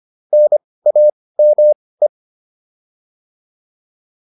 Morse code, heard as a keyed tone, about 25 wpm, spaced as for 18 wpm, 600 Hz. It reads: NAME
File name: PR305 18